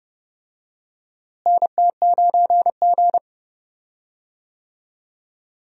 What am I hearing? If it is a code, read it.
Morse code, heard as a keyed tone, about 30 wpm, 700 Hz. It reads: NT9G
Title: USA Call Signs - 4 Characters - Repeated 2x - 30wpm